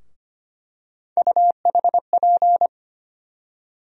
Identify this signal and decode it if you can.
Morse code, heard as a keyed tone, about 25 wpm, 700 Hz. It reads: UHP
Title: USA Call Sign Postfixes - 3 Letters 25wpm